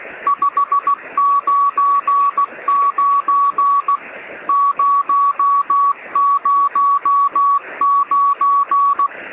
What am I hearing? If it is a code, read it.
Morse code, heard as a keyed tone, about 16 wpm, 1130 Hz. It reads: 599 009
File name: sm7lcb / 1814z sp3et 23cm